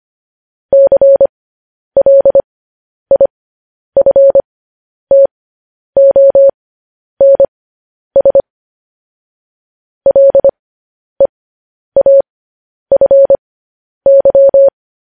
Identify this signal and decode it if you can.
Morse code, heard as a keyed tone, about 25 wpm, spaced as for 10 wpm, 550 Hz. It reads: CLIFTONS LEAFY
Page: News Headlines in Morse Code at 10 WPM